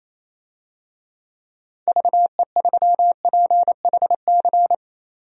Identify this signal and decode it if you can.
Morse code, heard as a keyed tone, about 28 wpm, 700 Hz. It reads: VE3PHC
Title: DXCC-100 Call Signs - 6 Characters - 28wpm